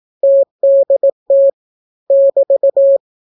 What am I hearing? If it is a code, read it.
Morse code, heard as a keyed tone, about 18 wpm, 550 Hz. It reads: TDT <BT>